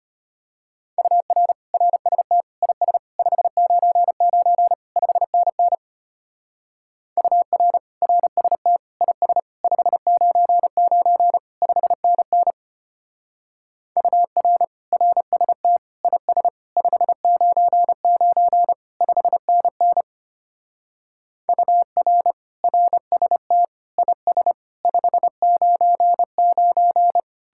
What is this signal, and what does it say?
Morse code, heard as a keyed tone, about 31 wpm, 700 Hz. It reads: UR RST IS 599 5NN UR RST IS 599 5NN UR RST IS 599 5NN UR RST IS 599